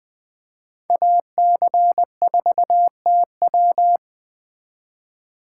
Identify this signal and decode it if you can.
Morse code, heard as a keyed tone, about 21 wpm, 700 Hz. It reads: AC4TW